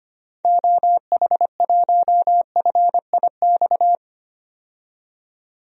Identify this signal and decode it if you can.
Morse code, heard as a keyed tone, about 25 wpm, 700 Hz. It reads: OH1FIX
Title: DXCC-100 Call Signs - 6 Characters - 25wpm